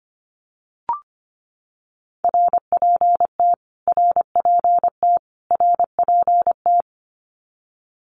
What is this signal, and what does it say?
Morse code, heard as a keyed tone, about 25 wpm, 700 Hz. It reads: RPT RPT RPT
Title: Abbreviations Repeated 3x - 25wpm